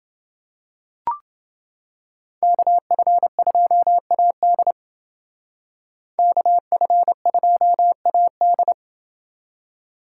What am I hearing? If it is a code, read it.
Morse code, heard as a keyed tone, about 29 wpm, 700 Hz. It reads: KF2AD KF2AD